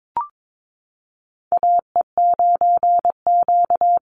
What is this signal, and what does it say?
Morse code, heard as a keyed tone, about 22 wpm, 700 Hz. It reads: AE9Q
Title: NA SOTA Chaser Callsigns 22wpm